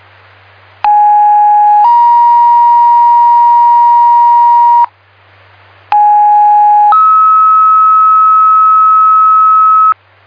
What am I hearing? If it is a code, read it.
Two-tone page, heard as a one-second tone, then a three-second tone, about 830 then 950 Hz, about 830 then 1290 Hz. It is heard from 0.8 to 9.9 s.